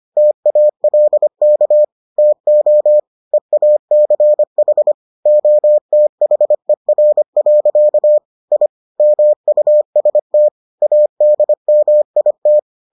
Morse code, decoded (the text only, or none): TALK TO EACH OTHER. I MUST ADMIT